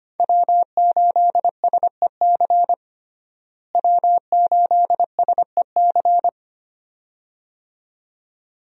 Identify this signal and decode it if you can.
Morse code, heard as a keyed tone, about 25 wpm, 700 Hz. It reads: W8SEC W8SEC
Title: USA Call Signs - 5 Characters - Repeated 2x - 25wpm